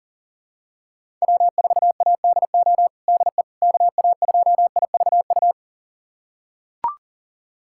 Morse code, heard as a keyed tone, about 40 wpm, 700 Hz. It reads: W4ADO DE KA2IVU
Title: USA Callsign DE Callsign 40wpm